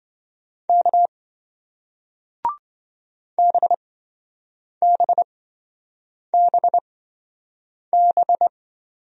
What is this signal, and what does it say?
Morse code, heard as a keyed tone, about 26 wpm, spaced as for 7 wpm, 700 Hz. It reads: K BBBB